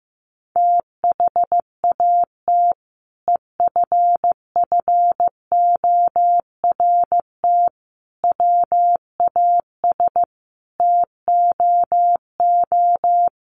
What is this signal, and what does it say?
Morse code, heard as a keyed tone, about 15 wpm, 700 Hz. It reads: THAT EFFORT WAS TOO
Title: Sentences from Top 1200 Words 15wpm